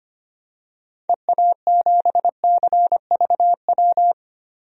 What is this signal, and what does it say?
Morse code, heard as a keyed tone, about 25 wpm, 700 Hz. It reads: EA7CVW
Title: DXCC-100 Call Signs - 6 Characters - Repeated 3x - 25wpm